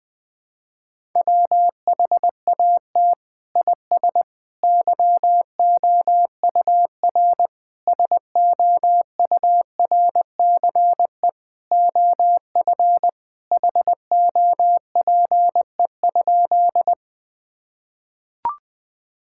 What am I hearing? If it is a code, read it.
Morse code, heard as a keyed tone, about 20 wpm, 700 Hz. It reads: WHAT IS YOUR SOURCE OF HOPE?
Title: Sentences from Top 1000 Words - Review Part 5 - 20wpm